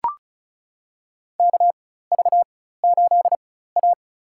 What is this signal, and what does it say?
Morse code, heard as a keyed tone, about 36 wpm, spaced as for 16 wpm, 700 Hz. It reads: KV8A
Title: USA Call Signs - 4 Characters - 4x Character Spacing - 35wpm